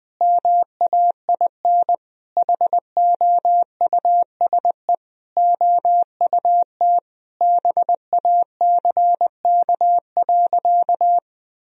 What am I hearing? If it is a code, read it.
Morse code, heard as a keyed tone, about 20 wpm, 700 Hz. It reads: MAIN HOUSE OUT BACK.
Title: Sentences from Top 1000 Words - Review Part 2 - 20wpm